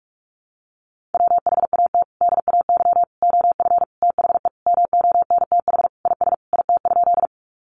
Morse code, heard as a keyed tone, about 44 wpm, 700 Hz. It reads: WHAT DAY OF THE MONTH IS IT?
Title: Sentences from Top 800 Words 45wpm